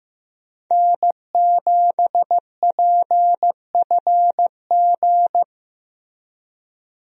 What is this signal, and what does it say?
Morse code, heard as a keyed tone, about 15 wpm, 700 Hz. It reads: N7PFG
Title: USA Call Signs - 5 Characters - Repeated 2x - 15wpm